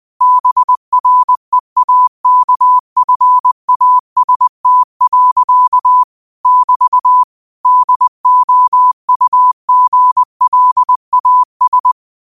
Morse code, heard as a keyed tone, about 20 wpm, 1000 Hz. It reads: BREAKFAST. <BT> DOUGLAS
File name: qotdfiles / 20wpm